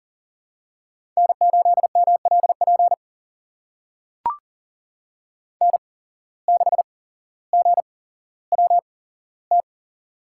Morse code, heard as a keyed tone, about 43 wpm, spaced as for 17 wpm, 700 Hz. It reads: N8MLP N6GWT
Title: USA Call Signs - 5 Characters - 8x Character Spacing - 40wpm